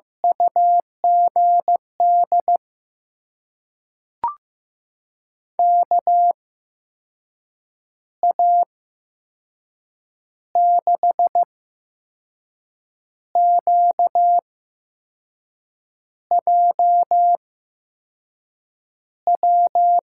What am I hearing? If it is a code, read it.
Morse code, heard as a keyed tone, about 15 wpm, spaced as for 5 wpm, 700 Hz. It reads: UGD KA6QJW
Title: USA Call Signs - 6 Characters - 8x Character Spacing - 15wpm